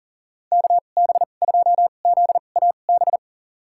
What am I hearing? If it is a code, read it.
Morse code, heard as a keyed tone, about 37 wpm, spaced as for 28 wpm, 700 Hz. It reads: KB2ZAB
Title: USA Call Signs - 2x Character Spacing - 40wpm